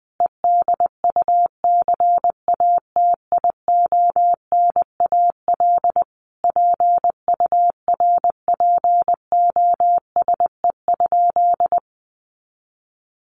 Morse code, heard as a keyed tone, about 20 wpm, 700 Hz. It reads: EDUCATIONAL PURPOSE?
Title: Sentences from Top 1200 Words 20wpm